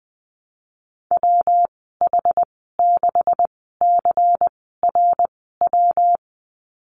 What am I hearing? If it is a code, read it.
Morse code, heard as a keyed tone, about 19 wpm, spaced as for 14 wpm, 700 Hz. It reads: WH6CRW
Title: 167 Lesson 27g - Number 7 - Call Signs 20wpm